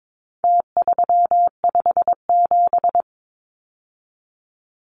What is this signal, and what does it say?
Morse code, heard as a keyed tone, about 22 wpm, 700 Hz. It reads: T357
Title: Numbers - Contest Style - 4-Digit 22wpm